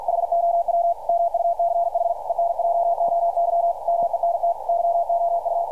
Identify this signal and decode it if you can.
Morse code, heard as a keyed tone, about 52 wpm, 680 Hz. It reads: HOW MUCH ENJOYMENT YOU
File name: archived / CRSnet